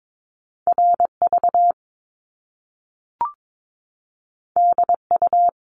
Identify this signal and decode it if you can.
Morse code, heard as a keyed tone, about 22 wpm, 700 Hz. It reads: RV DU
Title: Top Two Letter Combinations in Words 22wpm